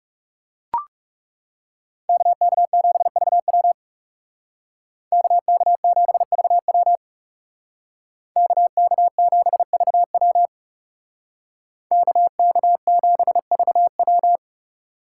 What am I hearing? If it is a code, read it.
Morse code, heard as a keyed tone, about 35 wpm, 700 Hz. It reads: KK7VW KK7VW KK7VW KK7VW
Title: Speed Racing - USA 2x2 Call Signs 30wpm